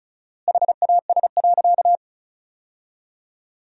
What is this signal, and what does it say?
Morse code, heard as a keyed tone, about 34 wpm, 700 Hz. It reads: HAS.
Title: 4x Word-Spacing - Sentences from Top 500 Words - 35wpm